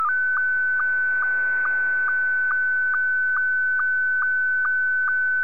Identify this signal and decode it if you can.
SSTV (slow-scan television), Scottie 1.